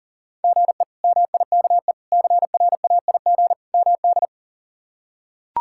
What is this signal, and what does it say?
Morse code, heard as a keyed tone, about 40 wpm, 700 Hz. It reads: GE MIKE CRAIG MD
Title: K1USN SST QSO Practice - 40wpm